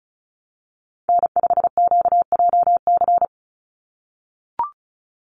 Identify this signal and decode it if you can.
Morse code, heard as a keyed tone, about 34 wpm, 700 Hz. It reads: N5QJC
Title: USA Call Signs 35wpm